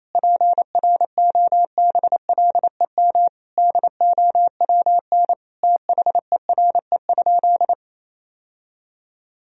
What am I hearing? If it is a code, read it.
Morse code, heard as a keyed tone, about 28 wpm, 700 Hz. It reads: PROBLEM DOWN THERE?
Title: Sentences from Top 1000 Words - Review Part 4 - 28wpm